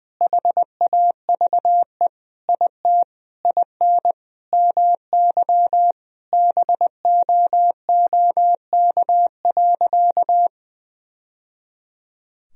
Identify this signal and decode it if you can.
Morse code, heard as a keyed tone, about 20 wpm, 700 Hz. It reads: HAVE IT IN MY BOOK.
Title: Sentences from Top 1000 Words - Review Part 1 - 20wpm